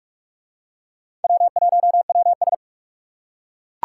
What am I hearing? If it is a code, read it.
Morse code, heard as a keyed tone, about 38 wpm, 700 Hz. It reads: W1WS